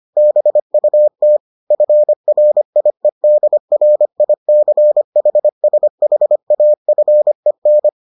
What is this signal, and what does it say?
Morse code, heard as a keyed tone, about 25 wpm, 600 Hz. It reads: BUT FRIEDRICHSHAFEN